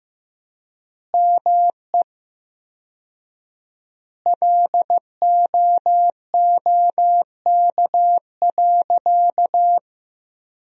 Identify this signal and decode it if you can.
Morse code, heard as a keyed tone, about 15 wpm, 700 Hz. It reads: ME LOOK.
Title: 4x Word-Spacing - Sentences from Top 300 Words - 15wpm